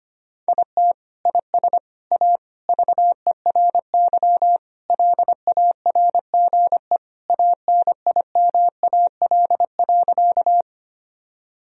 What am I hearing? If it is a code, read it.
Morse code, heard as a keyed tone, about 25 wpm, 700 Hz. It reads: IT IS A VERY LARGE ANIMAL.
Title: Sentences from Top 1000 Words - Review Part 1 - 25wpm